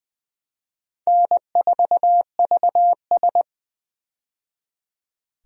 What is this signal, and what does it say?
Morse code, heard as a keyed tone, about 20 wpm, 700 Hz. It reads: N4VS
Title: USA Call Signs - 4 Characters - 20wpm